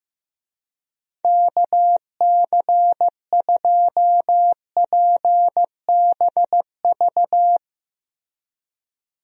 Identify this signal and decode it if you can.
Morse code, heard as a keyed tone, about 15 wpm, 700 Hz. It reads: KC2PBV
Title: USA Call Signs - 6 Characters - Repeated 3x - 15wpm